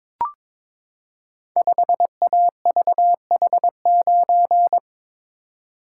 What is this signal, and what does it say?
Morse code, heard as a keyed tone, about 22 wpm, 700 Hz. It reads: 5AVH9
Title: Code Groups - 5 Letter:Numbers - 22wpm